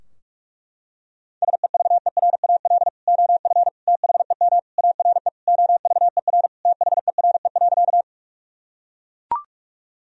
Morse code, heard as a keyed tone, about 42 wpm, 700 Hz. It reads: SEVERAL OF THEM ARE OVER THERE.
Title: Sentences from Top 1000 Words - Review Part 2 - 45wpm